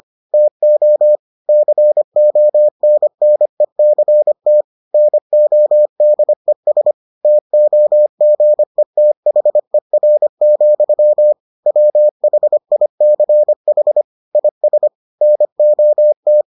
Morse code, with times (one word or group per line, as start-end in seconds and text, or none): 0.3-1.2 TO
1.5-4.6 CONNECT
4.9-6.9 NODES
7.2-11.3 TOGETHER,
11.7-14.0 WHICH
14.3-14.9 IS
15.2-16.4 NOT